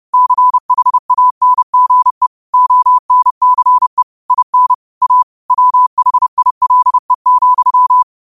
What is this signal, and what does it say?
Morse code, heard as a keyed tone, about 30 wpm, 1000 Hz. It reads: CHANGE ONCE IN A WHILE,